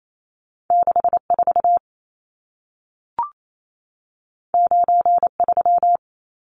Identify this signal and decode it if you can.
Morse code, heard as a keyed tone, about 28 wpm, 700 Hz. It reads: 64 93
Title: Numbers - 2-Digit 28wpm